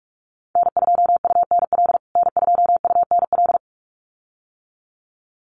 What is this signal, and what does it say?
Morse code, heard as a keyed tone, about 46 wpm, 700 Hz. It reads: N2UNL N2UNL
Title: USA Call Signs Repeated 2x 45wpm